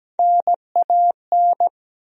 Morse code, heard as a keyed tone, about 17 wpm, 700 Hz. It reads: NAN
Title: USA Common Names - Max 3 Letters - 17wpm